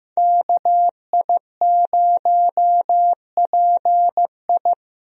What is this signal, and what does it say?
Morse code, heard as a keyed tone, about 15 wpm, 700 Hz. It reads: KI0PI
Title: USA Call Signs - 5 Characters - Repeated 2x - 15wpm